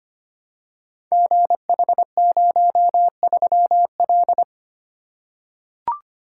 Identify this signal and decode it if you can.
Morse code, heard as a keyed tone, about 25 wpm, 700 Hz. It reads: GH03L
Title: Code Groups - 5 Letter:Numbers - 25wpm